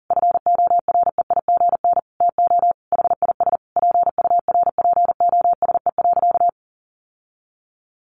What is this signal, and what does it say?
Morse code, heard as a keyed tone, about 40 wpm, 700 Hz. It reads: FOREIGN TO HIS PURPOSE.
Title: Sentences from Top 1400 Words 40wpm